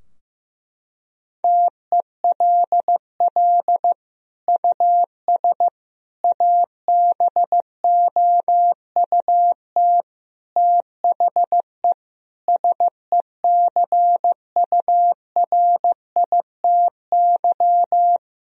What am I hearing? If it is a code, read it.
Morse code, heard as a keyed tone, about 15 wpm, 700 Hz. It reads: TELL US ABOUT THE SECURITY
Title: Sentences from Top 1000 Words - Review Part 4 - 15wpm